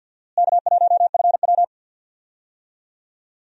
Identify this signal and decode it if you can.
Morse code, heard as a keyed tone, about 52 wpm, 700 Hz. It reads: K1FW
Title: USA Call Signs - 4 Characters - Repeated 3x - 50wpm